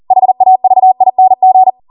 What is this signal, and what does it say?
Morse code, heard as a keyed tone, about 40 wpm, 750 Hz. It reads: HAVING